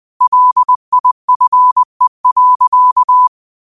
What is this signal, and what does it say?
Morse code, heard as a keyed tone, about 20 wpm, 1000 Hz. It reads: LIFE.